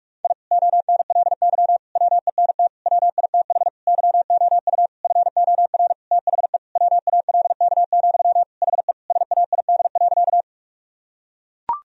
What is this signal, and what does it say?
Morse code, heard as a keyed tone, about 41 wpm, 700 Hz. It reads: I ONLY WENT WITH YOU FOR THE WALK, HE SAID.
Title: Sentences from Top 1000 Words - Review Part 1 - 45wpm